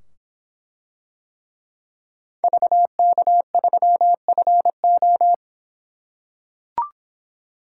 Morse code, heard as a keyed tone, about 25 wpm, 700 Hz. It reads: VK3FO